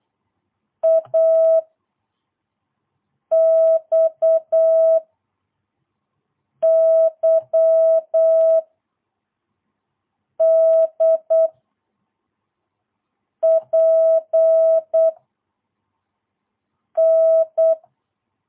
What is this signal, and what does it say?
Morse code, heard as a keyed tone, about 8 wpm, 650 Hz. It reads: A X Y D P N
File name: jh8cbh / 2022cw